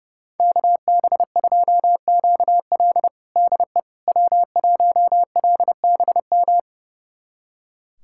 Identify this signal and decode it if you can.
Morse code, heard as a keyed tone, about 30 wpm, 700 Hz. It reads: KB2QL DE W1LBM
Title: USA Callsign DE Callsign - 5 Characters - 30wpm